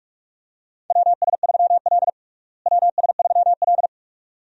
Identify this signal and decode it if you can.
Morse code, heard as a keyed tone, about 45 wpm, 700 Hz. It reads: WS3L WS3L
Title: USA Call Signs - 4 Characters - Repeated 2x - 45wpm